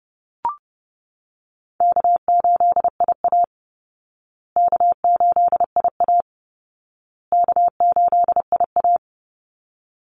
Morse code, heard as a keyed tone, about 30 wpm, 700 Hz. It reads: K8IA K8IA K8IA
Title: Easy Call Signs Repeated 3x - 30wpm